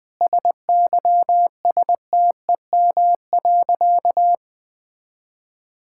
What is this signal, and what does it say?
Morse code, heard as a keyed tone, about 20 wpm, 700 Hz. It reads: SYSTEM.
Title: Sentences from Top 1000 Words - Review Part 5 - 20wpm